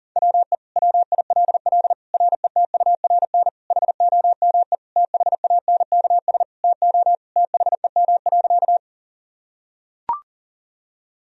Morse code, heard as a keyed tone, about 40 wpm, 700 Hz. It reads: WE WILL RETURN HOME THANKS TO THEM.
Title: Sentences from Top 1000 Words - Review Part 3 - 40wpm